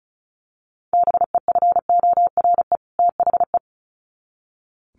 Morse code, heard as a keyed tone, about 34 wpm, 700 Hz. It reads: BEFORE THE
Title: Sets of 2 Words 35wpm